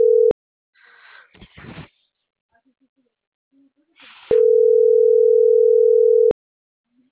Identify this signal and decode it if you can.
Ringback tone (North America), 440 + 480 Hz.